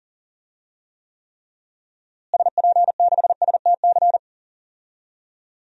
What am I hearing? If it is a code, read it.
Morse code, heard as a keyed tone, about 37 wpm, 700 Hz. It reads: SP6STC
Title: DXCC-100 Call Signs - 6 Characters - Repeated 3x - 40wpm